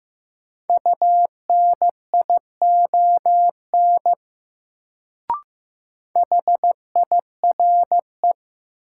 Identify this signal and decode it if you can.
Morse code, heard as a keyed tone, about 15 wpm, 700 Hz. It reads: UNION HIRE
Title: Sentences from Top 1400 Words - New Word Drill - 15wpm